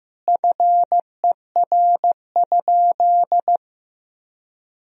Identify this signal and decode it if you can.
Morse code, heard as a keyed tone, about 16 wpm, 700 Hz. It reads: FER?
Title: Sentences from Top 100 Words and Abbreviations 15wpm